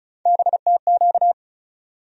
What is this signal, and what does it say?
Morse code, heard as a keyed tone, about 34 wpm, 700 Hz. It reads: BTQ
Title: USA Call Sign Postfixes - 3 Letters 35wpm